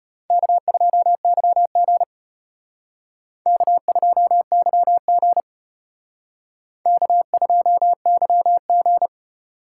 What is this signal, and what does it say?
Morse code, heard as a keyed tone, about 32 wpm, 700 Hz. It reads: K2YG K2YG K2YG